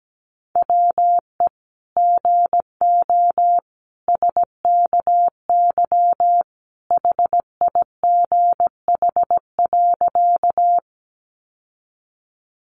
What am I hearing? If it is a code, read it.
Morse code, heard as a keyed tone, about 17 wpm, 700 Hz. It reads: WE GO SKY HIGH.
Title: Sentences from Top 1200 Words 17wpm